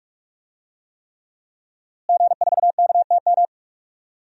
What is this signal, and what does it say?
Morse code, heard as a keyed tone, about 44 wpm, 700 Hz. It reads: G4KTM
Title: DXCC-100 Call Signs - 5 Characters - Repeated 3x - 45wpm